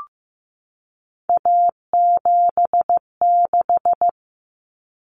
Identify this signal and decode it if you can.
Morse code, heard as a keyed tone, about 15 wpm, 700 Hz. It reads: A76
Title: Common Cut Numbers - 3-Digit 15wpm